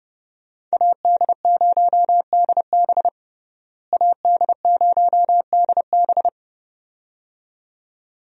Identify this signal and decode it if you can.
Morse code, heard as a keyed tone, about 32 wpm, 700 Hz. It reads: AD0DB AD0DB
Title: USA Call Signs - 5 Characters - Repeated 2x - 30wpm